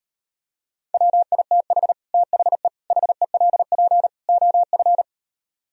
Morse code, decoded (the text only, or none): WITH THE HELP OF